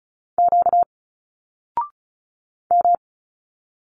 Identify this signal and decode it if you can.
Morse code, heard as a keyed tone, about 34 wpm, 700 Hz. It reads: Q M
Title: Single Letters 35wpm